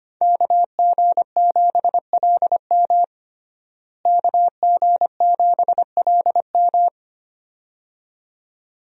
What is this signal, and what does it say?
Morse code, heard as a keyed tone, about 25 wpm, 700 Hz. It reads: KG7LM KG7LM
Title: USA Call Signs - 5 Characters - Repeated 2x - 25wpm